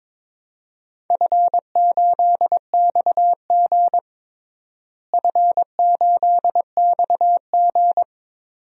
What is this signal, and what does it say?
Morse code, heard as a keyed tone, about 22 wpm, 700 Hz. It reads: F8XG F8XG
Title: DXCC-100 Call Signs - 4 Characters - Repeated 2x - 22wpm